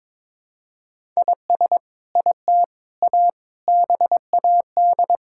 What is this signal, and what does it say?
Morse code, heard as a keyed tone, about 22 wpm, 700 Hz. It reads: IS IT A BAD
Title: Sentences from Top 1000 Words - Review Part 5 - 22wpm